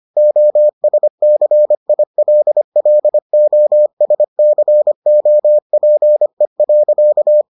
Morse code, heard as a keyed tone, about 25 wpm, 600 Hz. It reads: OSCILLOSCOPE.